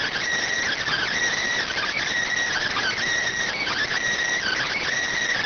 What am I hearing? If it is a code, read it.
Emergency Alert System (SAME) tones, heard as harsh buzzing bursts. It is heard from 0.7 to 4.8 s.